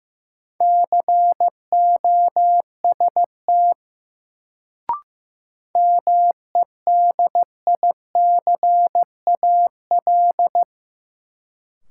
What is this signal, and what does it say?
Morse code, heard as a keyed tone, about 15 wpm, 700 Hz. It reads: COST MEDICAL
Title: Sentences from Top 800 Words - New Word Drill - 15wpm